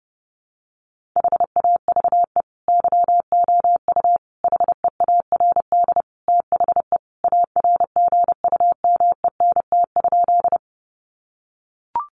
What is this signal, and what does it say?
Morse code, heard as a keyed tone, about 30 wpm, 700 Hz. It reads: HAVE YOU HEARD THE ARGUMENT?
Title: Sentences from Top 1400 Words 30wpm